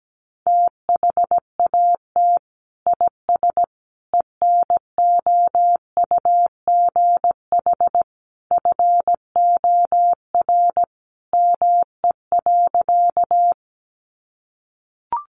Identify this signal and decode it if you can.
Morse code, heard as a keyed tone, about 17 wpm, 700 Hz. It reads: THAT IS ENOUGH FOR ME.
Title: Sentences from Top 400 Words 17wpm